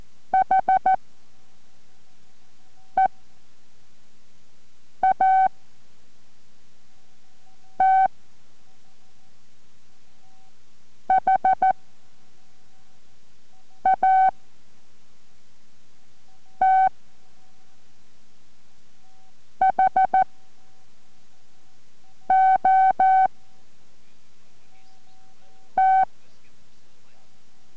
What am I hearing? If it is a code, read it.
Morse code, heard as a keyed tone, about 14 wpm, spaced as for 4 wpm, 760 Hz. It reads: HEAT HAT HOT